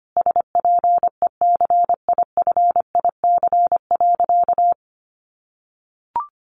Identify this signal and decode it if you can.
Morse code, heard as a keyed tone, about 25 wpm, 700 Hz. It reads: SPECIFIC.
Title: Sentences from Top 1300 Words 25wpm